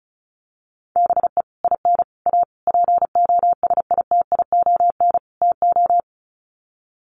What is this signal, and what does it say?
Morse code, heard as a keyed tone, about 34 wpm, 700 Hz. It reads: BE IN A POSITION TO
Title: Sets of 5 Words Encore2 35wpm